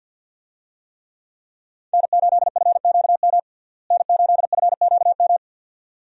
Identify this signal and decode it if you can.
Morse code, heard as a keyed tone, about 53 wpm, 700 Hz. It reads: N8FQM N8FQM
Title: USA Call Signs - 5 Characters - Repeated 2x - 50wpm